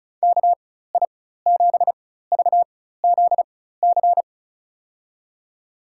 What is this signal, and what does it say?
Morse code, heard as a keyed tone, about 34 wpm, spaced as for 16 wpm, 700 Hz. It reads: KI7VZC
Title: USA Call Signs - 6 Characters - 4x Character Spacing - 35wpm